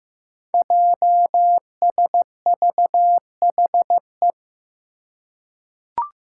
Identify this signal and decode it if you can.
Morse code, heard as a keyed tone, about 15 wpm, 700 Hz. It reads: JSVHE
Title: Five Challenging Letter-Number Combinations 15wpm